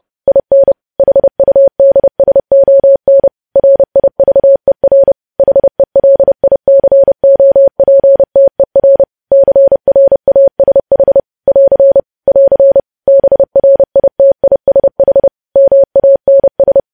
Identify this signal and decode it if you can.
Morse code, heard as a keyed tone, about 30 wpm, 550 Hz. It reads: IN HUDSON RIVER HELICOPTER CRASH <AR> <AR> BRITISH MANS
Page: news headlines in morse code at 30 wpm